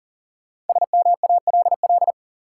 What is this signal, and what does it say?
Morse code, heard as a keyed tone, about 39 wpm, 700 Hz. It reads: SMALL
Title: Sentences from Top 200 Words - New Word Drill - 40wpm